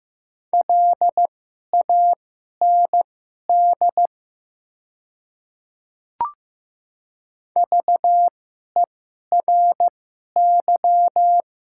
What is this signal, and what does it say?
Morse code, heard as a keyed tone, about 15 wpm, spaced as for 11 wpm, 700 Hz. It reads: LAND VERY
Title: 2x Character Spacing - Top 200 Words - 15wpm